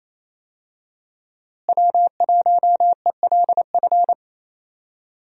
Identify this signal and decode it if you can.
Morse code, heard as a keyed tone, about 27 wpm, 700 Hz. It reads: W1ELF
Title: USA Call Signs - 5 Characters - Repeated 2x - 28wpm